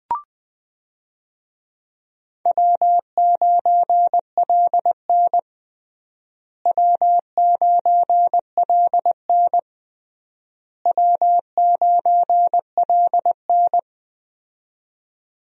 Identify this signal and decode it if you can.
Morse code, heard as a keyed tone, about 21 wpm, 700 Hz. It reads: W9LN W9LN W9LN
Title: USA Call Signs - 4 Characters - Repeated 3x - 20wpm